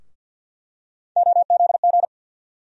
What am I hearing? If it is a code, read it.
Morse code, heard as a keyed tone, about 47 wpm, 700 Hz. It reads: OZG
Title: USA Call Sign Postfixes - 3 Letters 50wpm